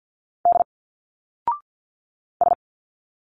Morse code, heard as a keyed tone, about 51 wpm, 700 Hz. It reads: D S
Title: Single Letters 50wpm